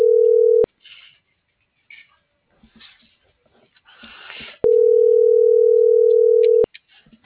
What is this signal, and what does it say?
Ringback tone (North America), 440 + 480 Hz.
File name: CENTRAL FRALSE / out brendaaviles.j - .88928